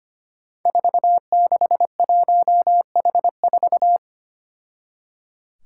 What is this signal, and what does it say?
Morse code, heard as a keyed tone, about 25 wpm, 700 Hz. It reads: 461H4
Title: Five Challenging Letter-Number Combinations 25wpm